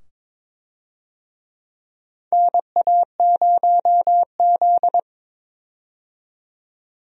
Morse code, heard as a keyed tone, about 22 wpm, 700 Hz. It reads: NA0Z
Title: USA Call Signs - 4 Characters - 22wpm